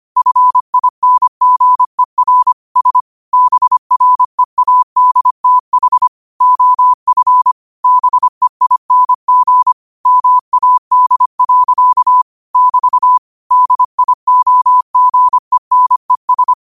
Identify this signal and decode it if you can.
Morse code, heard as a keyed tone, about 25 wpm, 1000 Hz. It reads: FINGER S BREADTH OF BEING MAD. <BT> DIOGENES
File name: qotdfiles / 25wpm